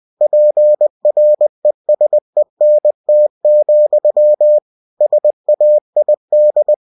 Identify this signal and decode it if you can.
Morse code, heard as a keyed tone, about 20 wpm, 600 Hz. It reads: PRESENT, SAID